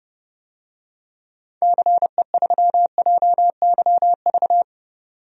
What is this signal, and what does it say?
Morse code, heard as a keyed tone, about 30 wpm, 700 Hz. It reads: CE3JYV
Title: DXCC-100 Call Signs - 6 Characters - 30wpm